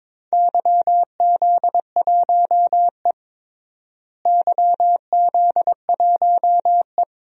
Morse code, heard as a keyed tone, about 22 wpm, 700 Hz. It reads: YZ1E YZ1E
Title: DXCC-100 Call Signs - 4 Characters - Repeated 3x - 22wpm